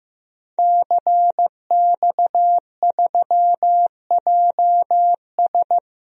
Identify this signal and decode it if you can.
Morse code, heard as a keyed tone, about 15 wpm, 700 Hz. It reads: CX3JS
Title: DXCC-100 Call Signs - 5 Characters - Repeated 2x - 15wpm